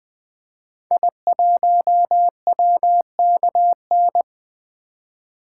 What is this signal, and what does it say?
Morse code, heard as a keyed tone, about 20 wpm, 700 Hz. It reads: I1WKN
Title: 2023 SOTA Activator Call Signs - 20wpm